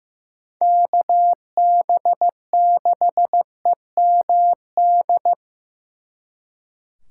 Morse code, heard as a keyed tone, about 18 wpm, 700 Hz. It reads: KB6EMD
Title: Speed Racing - USA 2x3 Call Signs 15wpm